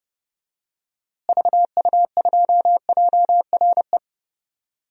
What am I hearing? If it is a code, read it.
Morse code, heard as a keyed tone, about 30 wpm, 700 Hz. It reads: VU2JRE
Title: DXCC-100 Call Signs - 6 Characters - Repeated 3x - 30wpm